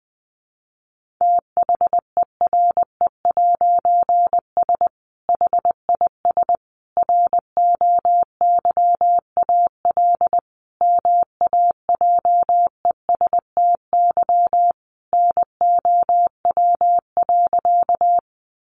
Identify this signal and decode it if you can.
Morse code, heard as a keyed tone, about 19 wpm, 700 Hz. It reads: THERE'S HIS ROYAL MAJESTY NOW.
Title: A Sound Of Thunder 20wpm